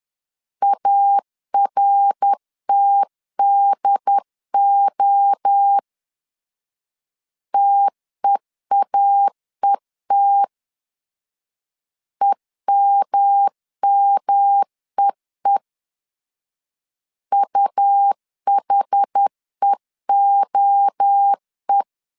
Morse code, heard as a keyed tone, about 10 wpm, 800 Hz. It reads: ARTDO TEAET EMMEE UHEOE
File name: CW / DET / 1 3 1500C S C 10wpm 3335